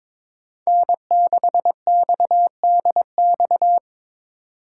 Morse code, heard as a keyed tone, about 22 wpm, 700 Hz. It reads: N6XDX
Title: USA Call Signs - 5 Characters - 6x Character Spacing - 22wpm